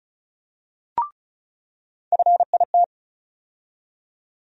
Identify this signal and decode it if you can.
Morse code, heard as a keyed tone, about 36 wpm, 700 Hz. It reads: FIT
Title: USA Call Sign Postfixes - 3 Letters 35wpm